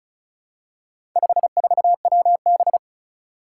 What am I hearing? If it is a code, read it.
Morse code, heard as a keyed tone, about 36 wpm, 700 Hz. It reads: 54WB
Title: Four Challenging Letter-Number Combinations 35wpm